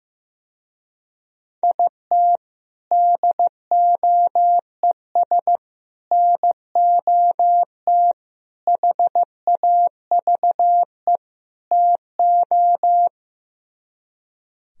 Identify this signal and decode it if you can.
Morse code, heard as a keyed tone, about 15 wpm, 700 Hz. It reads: IT DOES NOT HAVE TO
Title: Speed-Racing - Sets of 5 Words 15wpm